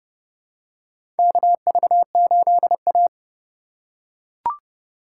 Morse code, heard as a keyed tone, about 30 wpm, 700 Hz. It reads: KV8A
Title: USA Call Signs - 4 Characters - 3x Character Spacing - 30wpm